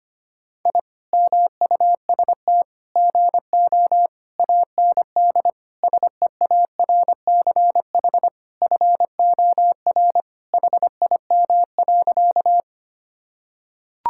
Morse code, heard as a keyed tone, about 25 wpm, 700 Hz. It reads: I MUST GO AND SEARCH FOR HIM.
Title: Sentences from Top 1000 Words - Review Part 5 - 25wpm